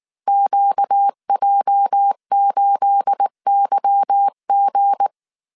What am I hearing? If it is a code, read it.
Morse code, heard as a keyed tone, about 19 wpm, 800 Hz. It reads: QJ8YG
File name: CW / DET / 1 9 1500C S A 18wpm 2459